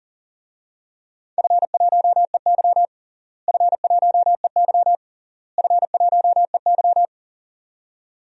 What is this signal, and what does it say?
Morse code, heard as a keyed tone, about 40 wpm, 700 Hz. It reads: F1EY F1EY F1EY
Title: DXCC-100 Call Signs - 4 Characters - Repeated 3x - 40wpm